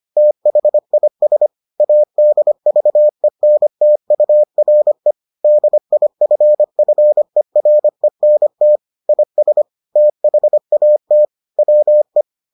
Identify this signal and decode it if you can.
Morse code, heard as a keyed tone, about 25 wpm, 600 Hz. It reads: THIS ADVENTURE DIFFERENT IS THAT WE